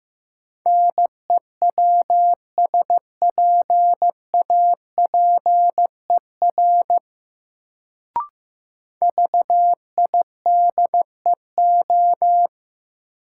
Morse code, heard as a keyed tone, about 15 wpm, 700 Hz. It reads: NEWSPAPER VIDEO
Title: Sentences from Top 1000 Words - Review Part 4 - New Word Drill - 15wpm